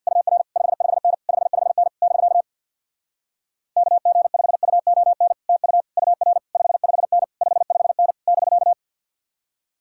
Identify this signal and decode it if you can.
Morse code, heard as a keyed tone, about 67 wpm, 700 Hz. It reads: UR 55N 55N <BK> KG5UON TU UR 55N 55N <BK>